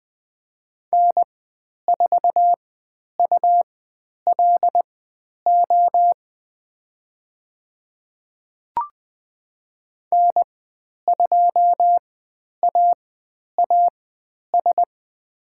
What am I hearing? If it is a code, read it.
Morse code, heard as a keyed tone, about 20 wpm, spaced as for 10 wpm, 700 Hz. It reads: N4ULO N2AAS
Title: 102 Lesson 17g - Letter u - Call Signs 20wpm